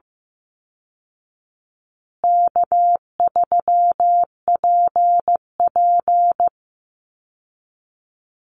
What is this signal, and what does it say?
Morse code, heard as a keyed tone, about 15 wpm, 700 Hz. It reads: K3PP
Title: Easy Call Signs Repeated 3x - 15wpm